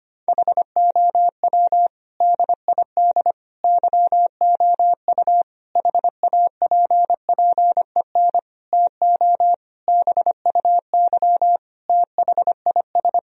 Morse code, heard as a keyed tone, about 25 wpm, 700 Hz. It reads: HOW DID YOU HAPPEN TO BUY THIS
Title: Sentences from Top 1000 Words - Review Part 4 - 25wpm